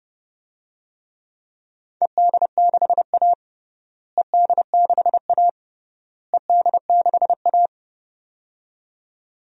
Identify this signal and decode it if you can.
Morse code, heard as a keyed tone, about 32 wpm, 700 Hz. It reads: ED6A ED6A ED6A
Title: DXCC-100 Call Signs - 4 Characters - Repeated 3x - 30wpm